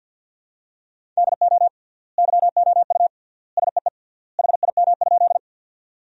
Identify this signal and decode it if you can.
Morse code, heard as a keyed tone, about 50 wpm, 700 Hz. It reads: DO YOU SEE HIM?
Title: 3x Word-Spacing - Sentences from Top 100 Words - 50wpm